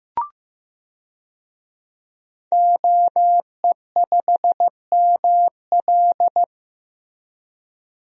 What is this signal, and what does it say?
Morse code, heard as a keyed tone, about 15 wpm, 700 Hz. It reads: OE5ML
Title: DXCC-100 Call Signs - 5 Characters - Repeated 3x - 15wpm